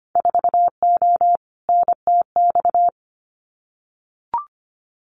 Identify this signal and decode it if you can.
Morse code, heard as a keyed tone, about 25 wpm, 700 Hz. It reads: 4O NTX
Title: Winter Field Day Exchanges 25wpm